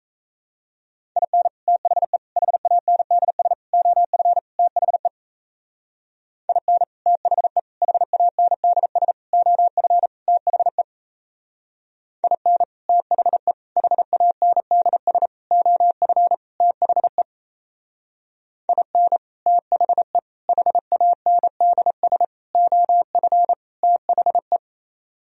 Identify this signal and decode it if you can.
Morse code, heard as a keyed tone, about 35 wpm, 700 Hz. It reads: IN THE HANDS OF THE IN THE HANDS OF THE IN THE HANDS OF THE IN THE HANDS OF THE